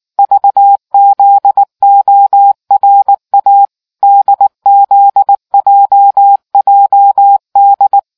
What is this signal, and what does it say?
Morse code, heard as a keyed tone, about 18 wpm, 800 Hz. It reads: VZORA DZJJD